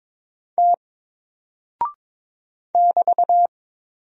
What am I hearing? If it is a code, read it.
Morse code, heard as a keyed tone, about 22 wpm, 700 Hz. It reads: T <BT>
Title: Single Character (Letter:Number:Punctuation:Prosign) 22wpm